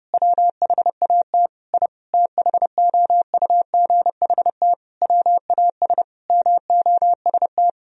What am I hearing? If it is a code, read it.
Morse code, heard as a keyed tone, about 30 wpm, 700 Hz. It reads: WHAT I THOUGHT WAS MOST
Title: Sentences from Top 1000 Words - Review Part 4 - 30wpm